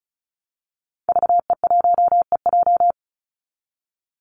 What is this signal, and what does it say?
Morse code, heard as a keyed tone, about 34 wpm, 700 Hz. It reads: VE1EJ
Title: Call Signs Repeated 3x - 35wpm